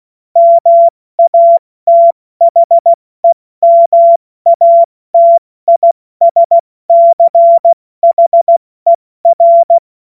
Morse code, heard as a keyed tone, about 16 wpm, spaced as for 14 wpm, 680 Hz. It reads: MATHEMATISCHER
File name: Morsekurs / Witze / Witz 21. Juni 2022 70 F80